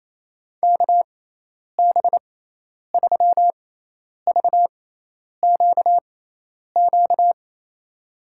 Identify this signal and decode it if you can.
Morse code, heard as a keyed tone, about 28 wpm, spaced as for 10 wpm, 700 Hz. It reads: KB3VQQ
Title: USA Call Signs - 6 Characters - 6x Character Spacing - 28wpm